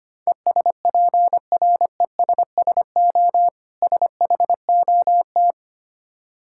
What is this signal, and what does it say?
Morse code, heard as a keyed tone, about 25 wpm, 700 Hz. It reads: ESPRESSO SHOT
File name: practice / 25wpm / Difficult Words with Hints - 25wpm